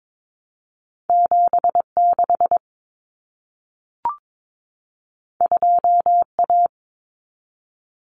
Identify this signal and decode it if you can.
Morse code, heard as a keyed tone, about 22 wpm, 700 Hz. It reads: 76 2A
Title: Common Cut Numbers - 2-Digit 22wpm